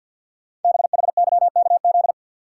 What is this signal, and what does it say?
Morse code, heard as a keyed tone, about 46 wpm, 700 Hz. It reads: BHYKZ
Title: Code Groups - 5 Letters - 50wpm